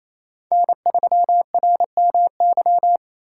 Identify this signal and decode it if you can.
Morse code, heard as a keyed tone, about 35 wpm, 700 Hz. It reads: N3RMY